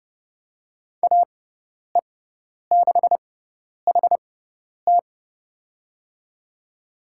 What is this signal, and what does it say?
Morse code, heard as a keyed tone, about 32 wpm, spaced as for 10 wpm, 690 Hz. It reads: AE6HT
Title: USA Call Signs - 5 Characters - 6x Character Spacing - 30wpm